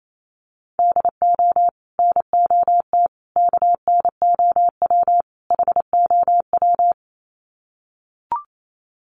Morse code, heard as a keyed tone, about 28 wpm, 700 Hz. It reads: DO NOT KNOW HOW
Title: Sets of 4 Words 28wpm